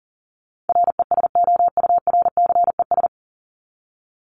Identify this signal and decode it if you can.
Morse code, heard as a keyed tone, about 40 wpm, 700 Hz. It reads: RESOURCES
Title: ES:S Suffix - Top 100 Words with Base Words 40wpm